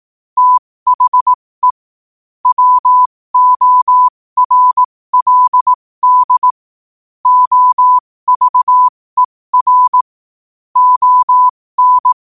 Morse code, heard as a keyed tone, about 18 wpm, spaced as for 15 wpm, 1000 Hz. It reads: THE WORLD OVER ON
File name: qotdfiles / 15wpm